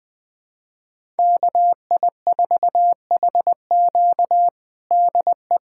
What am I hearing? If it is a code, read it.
Morse code, heard as a keyed tone, about 20 wpm, 700 Hz. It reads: KI4HQ DE
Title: USA Callsign DE Callsign 20wpm